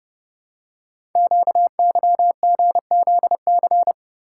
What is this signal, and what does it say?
Morse code, heard as a keyed tone, about 30 wpm, 700 Hz. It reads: QYGZC
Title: Code Groups - 5 Letters - 30wpm